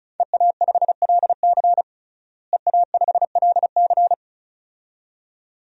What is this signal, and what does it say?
Morse code, heard as a keyed tone, about 36 wpm, 700 Hz. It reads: EA5LC EA5LC
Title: DXCC-100 Call Signs - 5 Characters - Repeated 2x - 35wpm